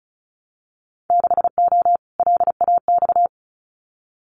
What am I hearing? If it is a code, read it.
Morse code, heard as a keyed tone, about 34 wpm, 700 Hz. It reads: 6O LAX
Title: Winter Field Day Exchanges Encore 35wpm